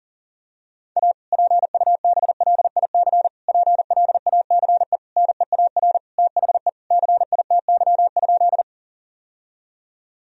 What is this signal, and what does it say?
Morse code, heard as a keyed tone, about 40 wpm, 700 Hz. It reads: A PUBLIC PLACE NEAR THE CITY?
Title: Sentences from Top 1000 Words - Review Part 3 - 40wpm